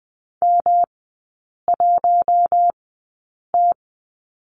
Morse code, heard as a keyed tone, about 20 wpm, spaced as for 8 wpm, 700 Hz. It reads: M1T
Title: 138 Lesson 23e - Letter f - Three Characters 20wpm